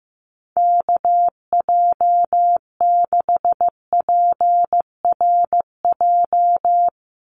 Morse code, heard as a keyed tone, about 15 wpm, 700 Hz. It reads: KJ6PRJ
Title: USA Call Signs 15wpm